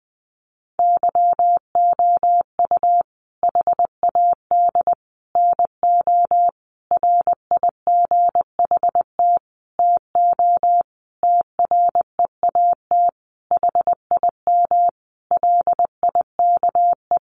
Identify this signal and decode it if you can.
Morse code, heard as a keyed tone, about 20 wpm, 700 Hz. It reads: YOU HAD NO RIGHT TO TREAT HIM LIKE
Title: Sentences from Top 1300 Words 20wpm